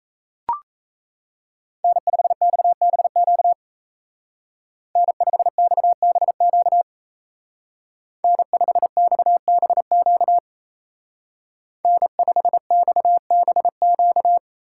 Morse code, heard as a keyed tone, about 34 wpm, 700 Hz. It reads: N5XBQ N5XBQ N5XBQ N5XBQ
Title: Speed Racing - USA Call Signs 28wpm